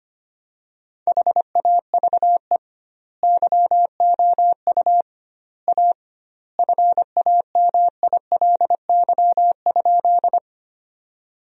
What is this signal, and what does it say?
Morse code, heard as a keyed tone, about 25 wpm, 700 Hz. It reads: HAVE YOU A FAMILY?
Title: 2x Word-Spacing - Sentences from Top 400 Words - 25wpm